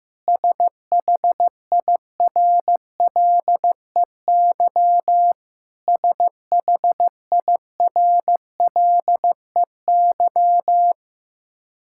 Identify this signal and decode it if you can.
Morse code, heard as a keyed tone, about 16 wpm, 700 Hz. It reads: SHIRLEY SHIRLEY
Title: K1USN SST QSO Practice 15wpm